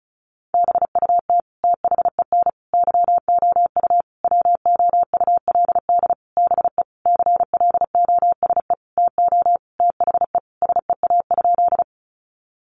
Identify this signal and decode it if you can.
Morse code, heard as a keyed tone, about 34 wpm, 700 Hz. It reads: BUT THEN YOU WOULD BE CLOSE TO THE SEA?
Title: Sentences from Top 300 Words 35wpm